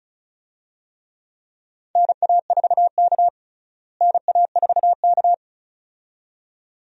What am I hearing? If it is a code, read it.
Morse code, heard as a keyed tone, about 34 wpm, 700 Hz. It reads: NA4K NA4K
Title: USA Call Signs - 4 Characters - Repeated 2x - 35wpm